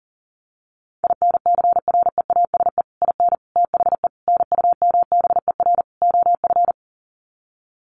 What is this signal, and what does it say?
Morse code, heard as a keyed tone, about 40 wpm, 700 Hz. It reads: INCREASE IN THE NUMBER OF
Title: Sets of 5 Words 40wpm